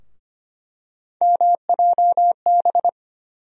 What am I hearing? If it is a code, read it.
Morse code, heard as a keyed tone, about 25 wpm, 700 Hz. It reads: MJB
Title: USA Call Sign Postfixes - 3 Letters 25wpm